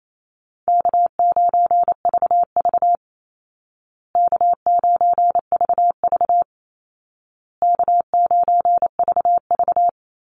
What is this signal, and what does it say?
Morse code, heard as a keyed tone, about 28 wpm, 700 Hz. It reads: K9VV K9VV K9VV
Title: Call Signs Repeated 2x - 28wpm